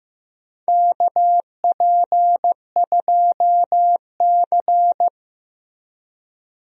Morse code, heard as a keyed tone, about 15 wpm, 700 Hz. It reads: KP2C
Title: DXCC-100 Call Signs - 4 Characters - 15wpm